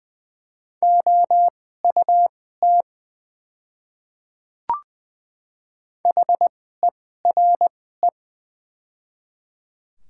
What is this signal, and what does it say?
Morse code, heard as a keyed tone, about 20 wpm, spaced as for 14 wpm, 700 Hz. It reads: OUT HERE
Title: 2x Character Spacing - Top 200 Words - 20wpm